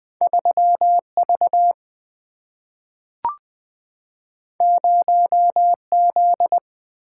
Morse code, heard as a keyed tone, about 20 wpm, 700 Hz. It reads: 3 V 0 Z
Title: Single-Letter Single-Number 20wpm